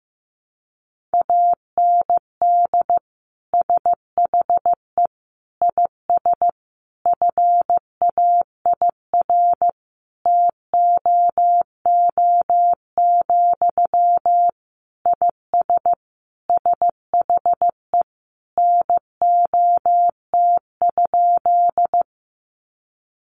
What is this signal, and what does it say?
Morse code, heard as a keyed tone, about 15 wpm, 700 Hz. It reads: AND SHE IS FAIR TOO, IS SHE NOT?
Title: Sentences from Top 1300 Words 15wpm